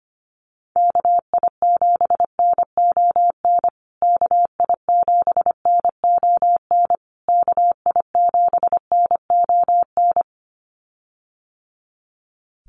Morse code, heard as a keyed tone, about 25 wpm, 700 Hz. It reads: KI7NON KI7NON KI7NON
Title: USA Call Signs Repeated 3x 25wpm